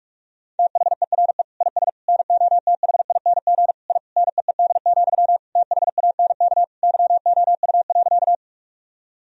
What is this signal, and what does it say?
Morse code, heard as a keyed tone, about 42 wpm, 700 Hz. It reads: THERE IS NOTHING I NEED, THANK YOU.
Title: Sentences from Top 1000 Words - Review Part 1 - 45wpm